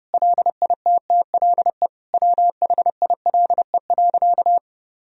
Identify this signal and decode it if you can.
Morse code, heard as a keyed tone, about 32 wpm, 700 Hz. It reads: LITTLE WHILE.
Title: Sentences from Top 1000 Words - Review Part 1 - 30wpm